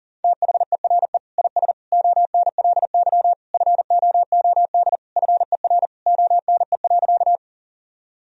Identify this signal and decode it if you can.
Morse code, heard as a keyed tone, about 40 wpm, 700 Hz. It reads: THERE IS ONLY FOOD FER ONE.
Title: Sentences from Top 300 Words and Abbreviations 40wpm